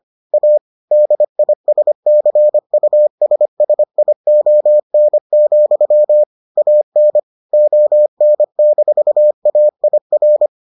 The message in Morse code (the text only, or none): A DISCUSSION, AN ON-AIR